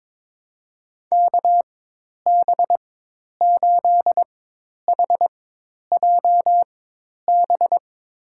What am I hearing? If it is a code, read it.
Morse code, heard as a keyed tone, about 22 wpm, spaced as for 10 wpm, 700 Hz. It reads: KB8HJB
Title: USA Call Signs - 6 Characters - 4x Character Spacing - 22wpm